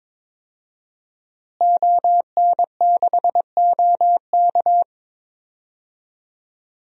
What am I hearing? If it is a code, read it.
Morse code, heard as a keyed tone, about 22 wpm, 700 Hz. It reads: ON6OK
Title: DXCC-100 Call Signs - 5 Characters - Repeated 2x - 22wpm